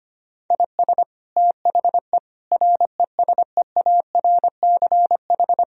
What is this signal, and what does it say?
Morse code, heard as a keyed tone, about 25 wpm, 700 Hz. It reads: IS THE RESEARCH
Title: Sentences from Top 1000 Words - Review Part 4 - 25wpm